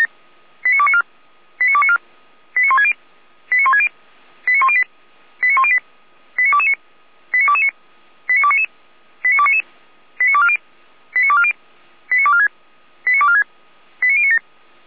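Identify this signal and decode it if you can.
Five-tone selective call (ZVEI), heard as quick runs of five tones, tones of about 70 ms. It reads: 89294 89294 89160 89160 89108 89108 89229 89229 89299 89299 89240 89240 89246 89246 89097
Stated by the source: Five-tone ZVEI selective call received by SDR for Feuerwehr Orlamünde, address 89277.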